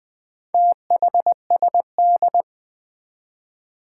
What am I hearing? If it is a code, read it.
Morse code, heard as a keyed tone, about 21 wpm, 700 Hz. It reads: THSD
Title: Four Challenging Letter-Number Combinations 20wpm